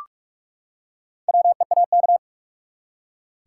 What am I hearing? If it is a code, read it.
Morse code, heard as a keyed tone, about 46 wpm, 700 Hz. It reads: WEAK
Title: Four Letter Words - 45wpm